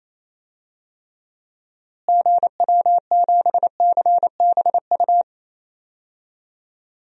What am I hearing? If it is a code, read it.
Morse code, heard as a keyed tone, about 28 wpm, 700 Hz. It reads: GW7CBU
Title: DXCC-100 Call Signs - 6 Characters - Repeated 2x - 28wpm